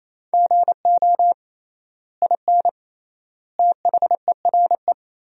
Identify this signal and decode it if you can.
Morse code, heard as a keyed tone, about 28 wpm, 700 Hz. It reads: GO IN THERE
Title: 3x Word-Spacing - Sentences from Top 500 Words - 28wpm